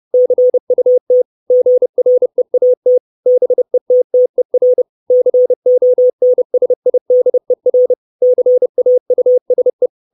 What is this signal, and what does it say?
Morse code, heard as a keyed tone, about 30 wpm, 500 Hz. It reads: CUT GREAT BETTER CONSIDER CAUSE
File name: 632 Words 30WPM